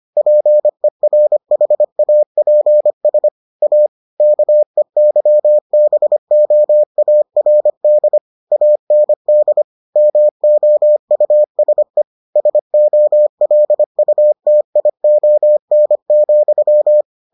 Morse code, heard as a keyed tone, about 25 wpm, 600 Hz. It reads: PERHAPS A KEYBOARD AND MOUSE SOLUTION,